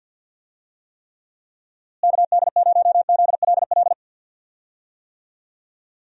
Morse code, heard as a keyed tone, about 51 wpm, 700 Hz. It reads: KD0ZLL
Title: USA Call Signs - 6 Characters - Repeated 2x - 50wpm